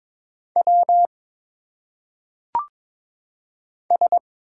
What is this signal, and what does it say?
Morse code, heard as a keyed tone, about 22 wpm, 700 Hz. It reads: W S
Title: Single Letters - Phonetic Pronunciation 22wpm